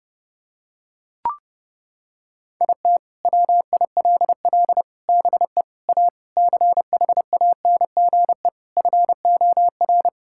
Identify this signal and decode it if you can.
Morse code, heard as a keyed tone, about 30 wpm, 700 Hz. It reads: IT WILL BE A CHANGE FOR
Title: Sentences from Top 1000 Words - Review Part 6 - 30wpm